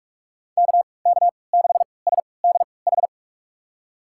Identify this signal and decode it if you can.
Morse code, heard as a keyed tone, about 47 wpm, spaced as for 26 wpm, 690 Hz. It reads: KK6SDH
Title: USA Call Signs - 6 Characters - 3x Character Spacing - 45wpm